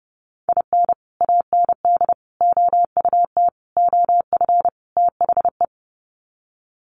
Morse code, heard as a keyed tone, about 30 wpm, 700 Hz. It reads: IN AND OUT OF THE
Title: Sets of 5 Words 30wpm